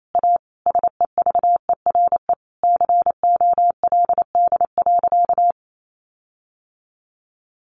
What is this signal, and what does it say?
Morse code, heard as a keyed tone, about 28 wpm, 700 Hz. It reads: A SEVERE COLD.
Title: Sentences from Top 1400 Words 28wpm